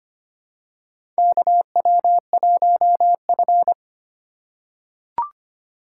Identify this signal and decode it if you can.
Morse code, heard as a keyed tone, about 26 wpm, 700 Hz. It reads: KW1F
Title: USA Call Signs - 4 Characters - 4x Character Spacing - 25wpm